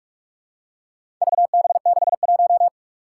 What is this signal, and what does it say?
Morse code, heard as a keyed tone, about 44 wpm, 700 Hz. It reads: VB61
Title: Four Challenging Letter-Number Combinations 45wpm